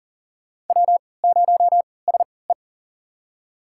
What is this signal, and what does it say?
Morse code, heard as a keyed tone, about 40 wpm, spaced as for 23 wpm, 700 Hz. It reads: W0SE
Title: USA Call Signs - 4 Characters - 3x Character Spacing - 40wpm